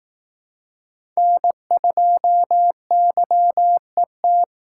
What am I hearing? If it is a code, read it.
Morse code, heard as a keyed tone, about 18 wpm, 700 Hz. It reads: N2YET
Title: Speed Racing - USA Call Signs 15wpm